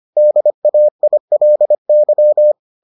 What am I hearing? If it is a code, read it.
Morse code, heard as a keyed tone, about 25 wpm, 600 Hz. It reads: DAILY